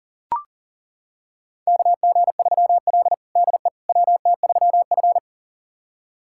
Morse code, heard as a keyed tone, about 40 wpm, 690 Hz. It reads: KG3L DE WT3F
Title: USA Callsign DE Callsign - 4 Characters - 40wpm